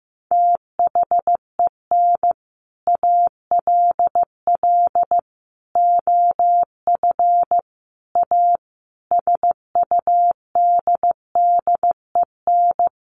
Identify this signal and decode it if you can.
Morse code, heard as a keyed tone, about 14 wpm, 700 Hz. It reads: THEN ALL OF A SUDDEN
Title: Sets of 5 Words Encore 15wpm